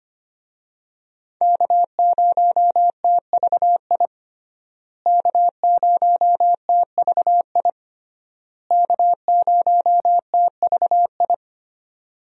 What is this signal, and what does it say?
Morse code, heard as a keyed tone, about 25 wpm, 700 Hz. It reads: K0TVI K0TVI K0TVI
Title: USA Call Signs - 5 Characters - Repeated 3x - 25wpm